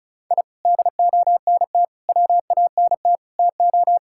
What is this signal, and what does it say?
Morse code, heard as a keyed tone, about 36 wpm, 700 Hz. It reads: I DONT WANT TO
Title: Sentences from Top 1000 Words - Review Part 5 - 35wpm